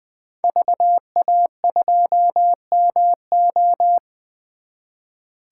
Morse code, heard as a keyed tone, about 20 wpm, 700 Hz. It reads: VA2MO
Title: 2023 SOTA Chaser Call Signs - 20wpm